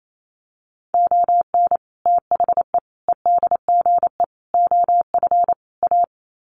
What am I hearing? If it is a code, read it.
Morse code, heard as a keyed tone, about 28 wpm, 700 Hz. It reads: ON THE EDGE OF A
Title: Sets of 5 Words Encore2 28wpm